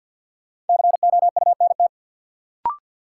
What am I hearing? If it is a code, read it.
Morse code, heard as a keyed tone, about 52 wpm, 700 Hz. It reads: COUNT
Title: Sentences from Top 1500 Words - New Word Drill Alt - 50wpm